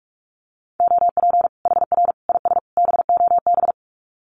Morse code, heard as a keyed tone, about 44 wpm, 700 Hz. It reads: OP HR IS BOB
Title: Common QSO Phrases 45wpm